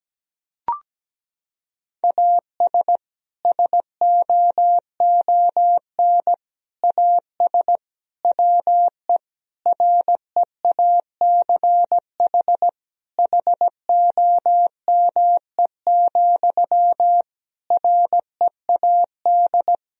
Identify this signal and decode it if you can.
Morse code, heard as a keyed tone, about 17 wpm, 700 Hz. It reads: AS SOON AS WE REACH HOME, READ
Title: Sentences from Top 1000 Words - Review Part 2 - 17wpm